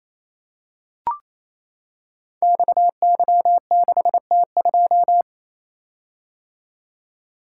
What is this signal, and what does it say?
Morse code, heard as a keyed tone, about 28 wpm, 700 Hz. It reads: XY6T2
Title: Code Groups - 5 Letter:Numbers - 28wpm